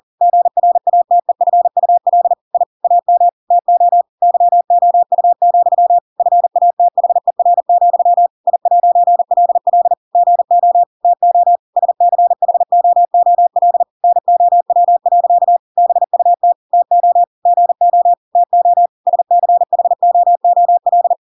Morse code, heard as a keyed tone, about 40 wpm, 700 Hz. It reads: GRATEFUL I AM TO YOU, FATHER, I'LL GO TO SCHOOL NOW. BUT TO GO TO SCHOOL
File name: Pinocchio / Pinocchio ch8 40 40 ik1hge